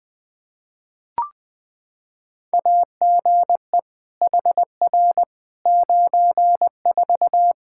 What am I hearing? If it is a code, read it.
Morse code, heard as a keyed tone, about 21 wpm, 700 Hz. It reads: AGE HR 94
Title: Practice QSO - Exchange 3 - More Info - 20wpm